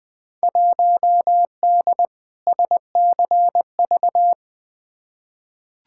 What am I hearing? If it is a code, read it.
Morse code, heard as a keyed tone, about 21 wpm, 700 Hz. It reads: 1D SCV
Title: Field Day Exchanges Encore - 20wpm